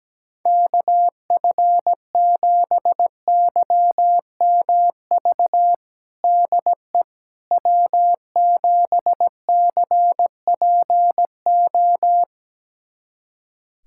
Morse code, heard as a keyed tone, about 17 wpm, 700 Hz. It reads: KF7YMV DE W7CPO
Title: USA Callsign DE Callsign 17wpm